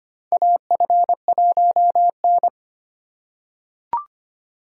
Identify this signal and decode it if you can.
Morse code, heard as a keyed tone, about 25 wpm, 700 Hz. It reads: AF1N
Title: USA Call Signs - 4 Characters - 4x Character Spacing - 25wpm